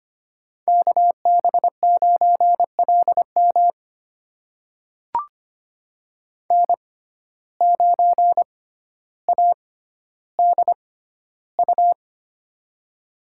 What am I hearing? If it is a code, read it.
Morse code, heard as a keyed tone, about 25 wpm, spaced as for 13 wpm, 700 Hz. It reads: KB9LM N9ADU
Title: USA Call Signs - 5 Characters - 6x Character Spacing - 25wpm